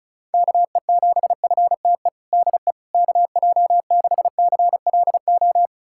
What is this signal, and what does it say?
Morse code, heard as a keyed tone, about 36 wpm, 700 Hz. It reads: KE7FTE DE KJ6CLO
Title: USA Callsign DE Callsign - 6 Characters - 35wpm